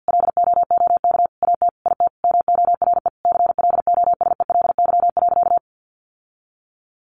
Morse code, heard as a keyed tone, about 50 wpm, 700 Hz. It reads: LOOK AT IT MORE CLOSELY.
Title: Sentences from Top 1400 Words 50wpm